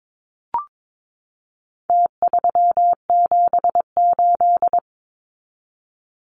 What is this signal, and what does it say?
Morse code, heard as a keyed tone, about 22 wpm, 700 Hz. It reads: T378
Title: Numbers - Contest Style - 4-Digit 22wpm